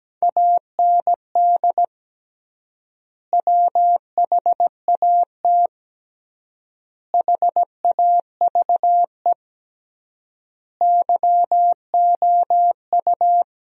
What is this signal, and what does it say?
Morse code, heard as a keyed tone, about 17 wpm, 700 Hz. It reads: AND WHAT HAVE YOU
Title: 3x Word-Spacing - Sentences from Top 500 Words - 17wpm